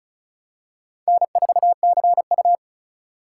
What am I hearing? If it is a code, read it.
Morse code, heard as a keyed tone, about 34 wpm, 700 Hz. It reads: N4CU
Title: USA Call Signs - 4 Characters - 2x Character Spacing - 35wpm